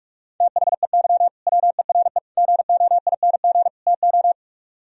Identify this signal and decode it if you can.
Morse code, heard as a keyed tone, about 45 wpm, 700 Hz. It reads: THEY WERE GOING TO
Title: Speed-Racing - Sets of 4 Words 45wpm